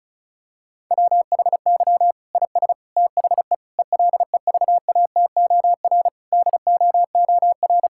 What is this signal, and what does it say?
Morse code, heard as a keyed tone, about 36 wpm, 700 Hz. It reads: WHY IS THE ELEVATOR DOOR
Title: Sentences from Top 1600 Words 35wpm